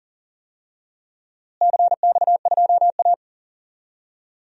Morse code, heard as a keyed tone, about 39 wpm, 700 Hz. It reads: CX2A
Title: DXCC-100 Call Signs - 4 Characters - Repeated 3x - 40wpm